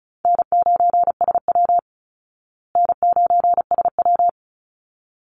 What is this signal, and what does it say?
Morse code, heard as a keyed tone, about 34 wpm, 700 Hz. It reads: N9SW N9SW
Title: Easy Call Signs Repeated 2x - 35wpm